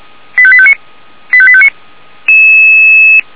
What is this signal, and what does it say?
Five-tone selective call (ZVEI), heard as quick runs of five tones, tones of about 70 ms. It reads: 85859 85859